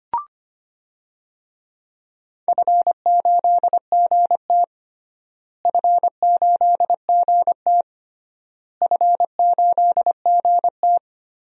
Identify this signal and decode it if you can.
Morse code, heard as a keyed tone, about 25 wpm, 700 Hz. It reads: F8GT F8GT F8GT
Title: DXCC-100 Call Signs - 4 Characters - Repeated 3x - 25wpm